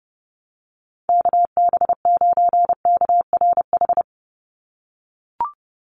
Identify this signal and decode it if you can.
Morse code, heard as a keyed tone, about 30 wpm, 700 Hz. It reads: KB9KRH
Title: USA Call Signs 30wpm